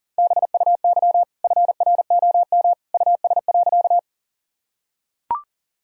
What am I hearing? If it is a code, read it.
Morse code, heard as a keyed tone, about 40 wpm, 700 Hz. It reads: BUY FROM US.
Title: Sentences from Top 1000 Words - Review Part 4 - 40wpm